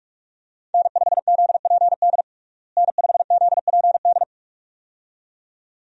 Morse code, heard as a keyed tone, about 42 wpm, 700 Hz. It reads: N5ZPD N5ZPD
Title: USA Call Signs - 5 Characters - Repeated 2x - 45wpm